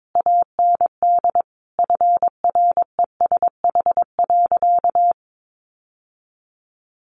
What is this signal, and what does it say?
Morse code, heard as a keyed tone, about 22 wpm, 700 Hz. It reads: AND FRESH.
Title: Sentences from Top 1500 Words 22wpm